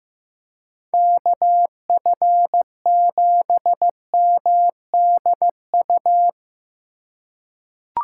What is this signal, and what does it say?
Morse code, heard as a keyed tone, about 15 wpm, 700 Hz. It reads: KF7MDU
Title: USA Call Signs - 6 Characters - 4x Character Spacing - 15wpm